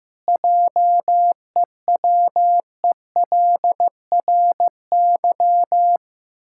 Morse code, heard as a keyed tone, about 15 wpm, 700 Hz. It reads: JEWELRY
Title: Pangram Challenge v2 - 15wpm